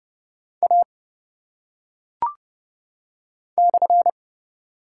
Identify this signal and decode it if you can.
Morse code, heard as a keyed tone, about 30 wpm, 700 Hz. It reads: A /
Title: Top 180 QSO Elements 30wpm